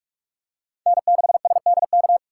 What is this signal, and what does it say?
Morse code, heard as a keyed tone, about 46 wpm, 700 Hz. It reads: N6SDK
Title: USA Call Signs - 3x Character Spacing - 45wpm